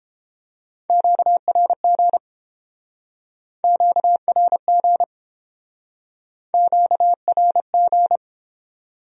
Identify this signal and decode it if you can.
Morse code, heard as a keyed tone, about 29 wpm, 700 Hz. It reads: QRG QRG QRG